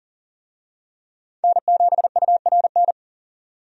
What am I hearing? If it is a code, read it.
Morse code, heard as a keyed tone, about 40 wpm, 700 Hz. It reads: N7URN
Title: USA Call Signs - 5 Characters - 40wpm